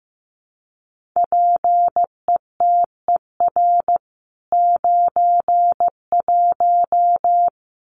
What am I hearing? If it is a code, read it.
Morse code, heard as a keyed tone, about 15 wpm, 700 Hz. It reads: PETER 91
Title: CWT Exchange 15wpm